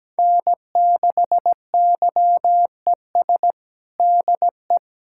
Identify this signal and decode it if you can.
Morse code, heard as a keyed tone, about 17 wpm, 700 Hz. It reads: N6YES DE
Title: Practice QSO - Exchange 4 - Ending - 17wpm